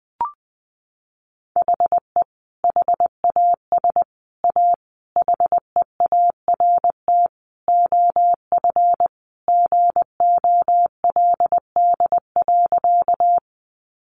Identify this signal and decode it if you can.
Morse code, heard as a keyed tone, about 20 wpm, 700 Hz. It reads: HE HAS A HEART OF GOLD.
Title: Sentences from Top 1200 Words 20wpm